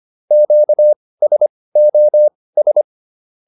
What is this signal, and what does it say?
Morse code, heard as a keyed tone, about 25 wpm, spaced as for 18 wpm, 600 Hz. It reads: QSOS